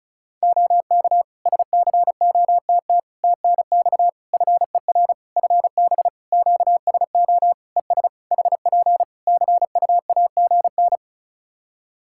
Morse code, heard as a keyed tone, about 36 wpm, 700 Hz. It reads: OK SCOTT TNX FER FB QSO ES HP CUAGN
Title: Practice QSO - Exchange 4 - Ending - 35wpm